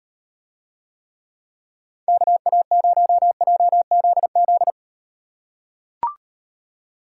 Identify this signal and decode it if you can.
Morse code, heard as a keyed tone, about 35 wpm, 700 Hz. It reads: KA0JZZ